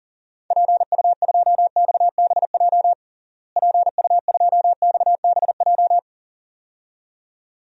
Morse code, heard as a keyed tone, about 40 wpm, 700 Hz. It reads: PU2XBJ PU2XBJ
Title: DXCC-100 Call Signs - 6 Characters - Repeated 2x - 40wpm